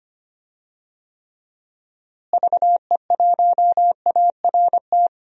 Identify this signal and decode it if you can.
Morse code, heard as a keyed tone, about 25 wpm, 700 Hz. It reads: VE1ART
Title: DXCC-100 Call Signs - 6 Characters - Repeated 3x - 25wpm